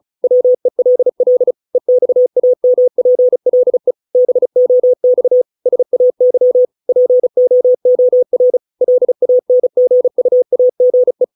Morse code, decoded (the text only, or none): WELL EXAMPLE BOX SAY POOR LANGUAGE